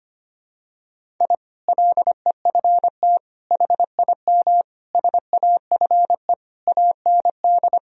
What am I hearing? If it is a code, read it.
Morse code, heard as a keyed tone, about 25 wpm, 700 Hz. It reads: I LEFT HIM SAFE AND
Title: Sentences from Top 1000 Words - Review Part 1 - 25wpm